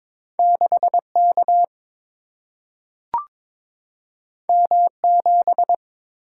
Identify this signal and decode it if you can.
Morse code, heard as a keyed tone, about 22 wpm, 700 Hz. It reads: 6 K M 7
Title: Single-Letter Single-Number 22wpm